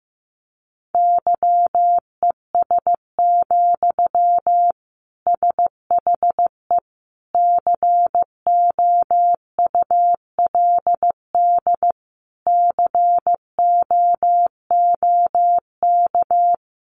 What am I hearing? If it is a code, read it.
Morse code, heard as a keyed tone, about 14 wpm, 700 Hz. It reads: YES, SHE COULD COOK
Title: Sentences from Top 700 Words 15wpm